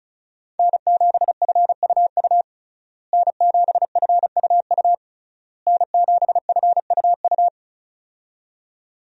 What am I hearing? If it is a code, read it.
Morse code, heard as a keyed tone, about 34 wpm, 700 Hz. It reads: N7FUU N7FUU N7FUU
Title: USA Call Signs - 5 Characters - Repeated 3x - 35wpm